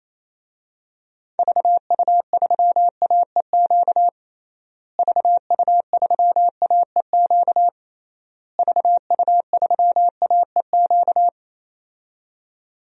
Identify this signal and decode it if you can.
Morse code, heard as a keyed tone, about 28 wpm, 700 Hz. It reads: VU3AEQ VU3AEQ VU3AEQ
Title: DXCC-100 Call Signs - 6 Characters - Repeated 3x - 28wpm